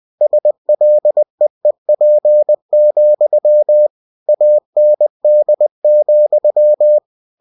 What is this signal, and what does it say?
Morse code, heard as a keyed tone, about 20 wpm, 600 Hz. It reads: SLEEP, AND,
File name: cw-books / winnie-the-pooh / winnie-20wpm-page-0039